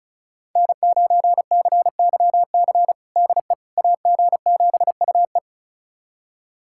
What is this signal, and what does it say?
Morse code, heard as a keyed tone, about 36 wpm, 700 Hz. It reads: N9CYC DE AG7UE
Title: USA Callsign DE Callsign - 5 Characters - 35wpm